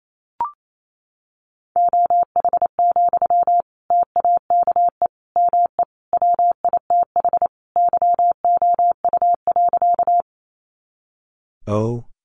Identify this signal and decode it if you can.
Morse code, heard as a keyed tone, about 28 wpm, 700 Hz. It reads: OH, TAKE ME WITH YOU.
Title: Sentences from Top 500 Words 28wpm